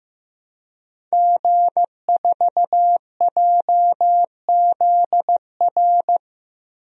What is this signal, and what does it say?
Morse code, heard as a keyed tone, about 16 wpm, 700 Hz. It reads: G4JZR
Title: DXCC-100 Call Signs - 5 Characters - Repeated 3x - 15wpm